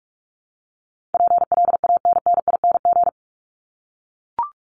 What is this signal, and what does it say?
Morse code, heard as a keyed tone, about 44 wpm, 700 Hz. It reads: PLANNING
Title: ING Suffix - Top 100 Words with Base Words 45wpm